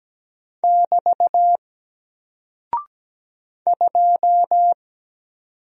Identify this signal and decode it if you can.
Morse code, heard as a keyed tone, about 17 wpm, 700 Hz. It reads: <BT> 2
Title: Single Character (Letter:Number:Punctuation:Prosign) Alt 17wpm